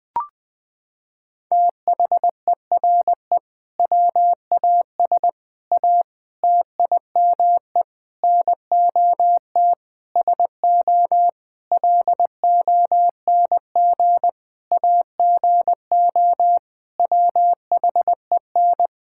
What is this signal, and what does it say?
Morse code, heard as a keyed tone, about 20 wpm, 700 Hz. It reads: THERE WAS A TIME NOT SO LONG AGO WHEN
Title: Sentences with 8-Word Phrases - 20wpm